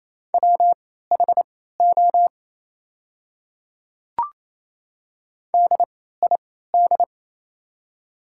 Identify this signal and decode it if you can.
Morse code, heard as a keyed tone, about 28 wpm, spaced as for 16 wpm, 700 Hz. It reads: WHO DID
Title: 3x Character Spacing - Top 100 Words - 28wpm